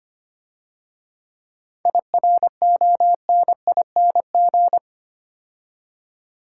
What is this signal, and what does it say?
Morse code, heard as a keyed tone, about 28 wpm, 690 Hz. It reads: IRONING
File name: Easy Words - Speed-Racing with Hints - 25wpm